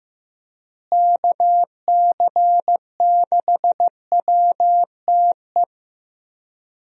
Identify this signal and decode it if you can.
Morse code, heard as a keyed tone, about 16 wpm, 700 Hz. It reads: KC6WTE
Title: USA Call Signs - 6 Characters - Repeated 2x - 15wpm